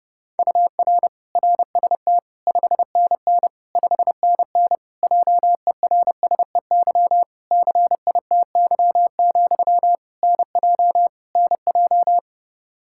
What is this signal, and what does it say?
Morse code, heard as a keyed tone, about 30 wpm, 700 Hz. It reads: UR RST 5NN 5NN JERSEY CITY, NJ NJ
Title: Practice QSO - Exchange 1 - Beginning - 30wpm